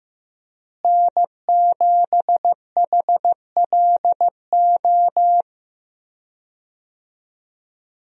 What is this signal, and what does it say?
Morse code, heard as a keyed tone, about 15 wpm, 700 Hz. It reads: N7HLO
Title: USA Call Signs - 5 Characters - Repeated 2x - 15wpm